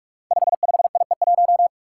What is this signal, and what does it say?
Morse code, heard as a keyed tone, about 44 wpm, 700 Hz. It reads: 55IE1
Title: Five Challenging Letter-Number Combinations 45wpm